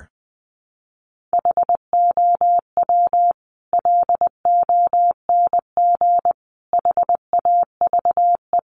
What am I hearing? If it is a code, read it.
Morse code, heard as a keyed tone, about 20 wpm, 700 Hz. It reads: HOW LONG HAVE
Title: Sentences from Top 200 Words 20wpm